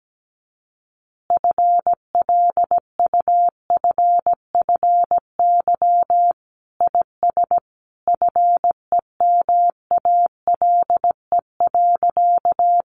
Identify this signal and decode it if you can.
Morse code, heard as a keyed tone, about 17 wpm, 700 Hz. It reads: FLUFFY IS FEMALE.
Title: Easy Phrases 17wpm